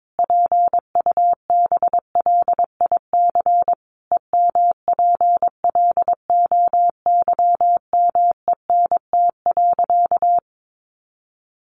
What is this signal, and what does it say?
Morse code, heard as a keyed tone, about 22 wpm, 700 Hz. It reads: PUBLIC EMPLOYMENT.
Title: Sentences from Top 1500 Words 22wpm